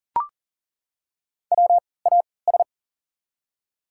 Morse code, heard as a keyed tone, about 40 wpm, spaced as for 23 wpm, 700 Hz. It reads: WAS
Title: 3x Character Spacing - Top 100 Words - 40wpm